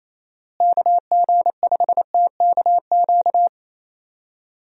Morse code, heard as a keyed tone, about 28 wpm, 700 Hz. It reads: KG5TKQ
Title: USA Call Signs - 6 Characters - Repeated 2x - 28wpm